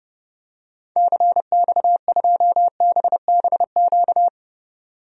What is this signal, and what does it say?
Morse code, heard as a keyed tone, about 30 wpm, 700 Hz. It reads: CX2BBQ
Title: DXCC-100 Call Signs - 6 Characters - 30wpm